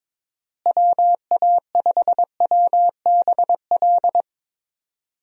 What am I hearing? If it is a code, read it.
Morse code, heard as a keyed tone, about 21 wpm, 700 Hz. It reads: WA5WBL
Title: USA Call Signs - 6 Characters - Repeated 3x - 22wpm